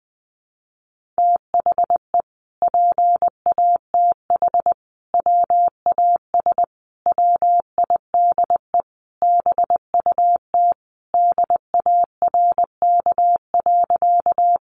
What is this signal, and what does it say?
Morse code, heard as a keyed tone, about 20 wpm, 700 Hz. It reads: THE PATH WAS WIDE BUT DARK.
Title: Sentences from Top 1400 Words 20wpm